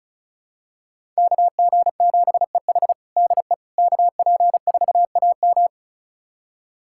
Morse code, heard as a keyed tone, about 36 wpm, 700 Hz. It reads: KG7EH DE KP4AM
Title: USA Callsign DE Callsign - 5 Characters - 35wpm